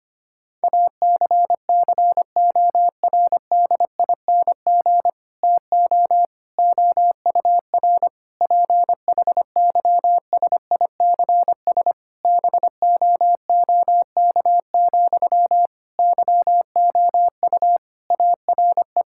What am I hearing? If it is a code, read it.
Morse code, heard as a keyed tone, about 25 wpm, 700 Hz. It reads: ACCORDING TO OUR PHYSICS BOOK, YOU ARE
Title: Sentences from Top 1000 Words - Review Part 5 - 25wpm